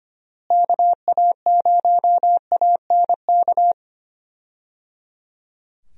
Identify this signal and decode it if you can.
Morse code, heard as a keyed tone, about 25 wpm, 700 Hz. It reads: KA0ANK
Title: USA Call Signs - 6 Characters - Repeated 3x - 25wpm